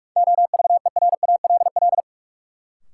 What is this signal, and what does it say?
Morse code, heard as a keyed tone, about 42 wpm, 700 Hz. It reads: OVERALL
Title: Sentences from Top 1000 Words - Review Part 7 - New Word Drill - 45wpm